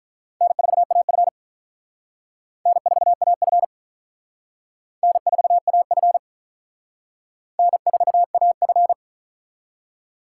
Morse code, heard as a keyed tone, about 44 wpm, 700 Hz. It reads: N4AF N4AF N4AF N4AF